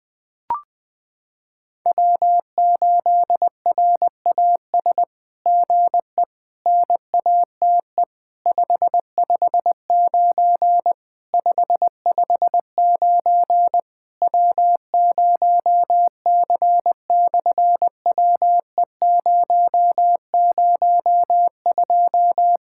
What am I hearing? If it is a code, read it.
Morse code, heard as a keyed tone, about 20 wpm, 700 Hz. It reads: W8RAS GE NATE 559 559 W0C/WE002
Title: SOTA QSO Practice 20wpm